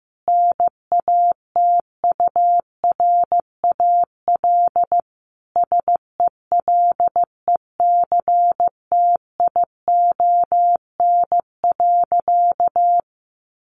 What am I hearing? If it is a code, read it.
Morse code, heard as a keyed tone, about 14 wpm, 700 Hz. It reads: NATURAL SELECTION.
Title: Sentences from Top 1400 Words 15wpm